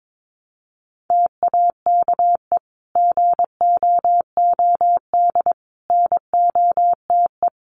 Morse code, heard as a keyed tone, about 22 wpm, 700 Hz. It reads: TAKE GOOD NOTE
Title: Sentences from Top 500 Words 22wpm